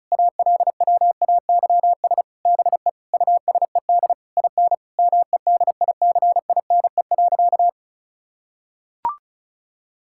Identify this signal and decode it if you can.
Morse code, heard as a keyed tone, about 36 wpm, 700 Hz. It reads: ALWAYS BE USED IN MEDICINE.
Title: Sentences from Top 1000 Words - Review Part 6 - 35wpm